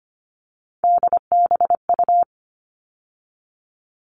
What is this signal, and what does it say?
Morse code, heard as a keyed tone, about 25 wpm, 700 Hz. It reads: DBU
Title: All Cut Numbers - 3-Digit 25wpm